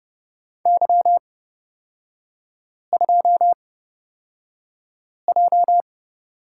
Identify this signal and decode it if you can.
Morse code, heard as a keyed tone, about 29 wpm, spaced as for 5 wpm, 700 Hz. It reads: Y2J
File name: Lesson 36e - BT - Three Characters 30wpm